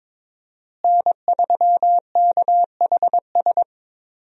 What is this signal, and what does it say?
Morse code, heard as a keyed tone, about 22 wpm, 700 Hz. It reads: N3KHS
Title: USA Call Signs - 5 Characters - Repeated 3x - 22wpm